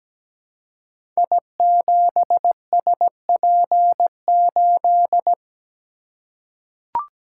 Morse code, heard as a keyed tone, about 17 wpm, 690 Hz. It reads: I7SP8
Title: Code Groups - 5 Letter:Numbers - 17wpm